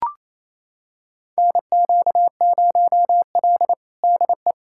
Morse code, heard as a keyed tone, about 28 wpm, 700 Hz. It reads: NQ0L DE
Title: USA Callsign DE Callsign - 4 Characters - 28wpm